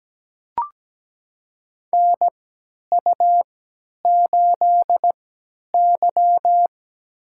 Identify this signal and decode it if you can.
Morse code, heard as a keyed tone, about 17 wpm, spaced as for 10 wpm, 700 Hz. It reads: NU8Y
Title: USA Call Signs - 3x Character Spacing - 17wpm